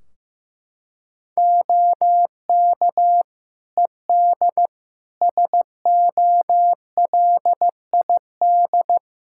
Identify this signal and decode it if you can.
Morse code, heard as a keyed tone, about 15 wpm, 700 Hz. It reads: OK ED SOLID
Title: Practice QSO - Exchange 3 - More Info - 15wpm